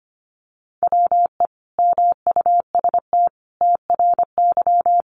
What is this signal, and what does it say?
Morse code, heard as a keyed tone, about 25 wpm, 700 Hz. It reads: WE MUST TRY
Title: Sentences from Top 1200 Words 25wpm